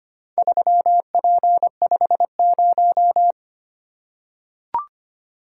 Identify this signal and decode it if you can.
Morse code, heard as a keyed tone, about 25 wpm, 700 Hz. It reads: 3P50
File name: practice / 25wpm / Four Letter-Number Combinations - 25wpm